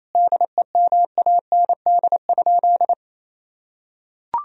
Morse code, heard as a keyed tone, about 28 wpm, 700 Hz. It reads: DEMAND?
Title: Sentences from Top 1000 Words - Review Part 4 - 28wpm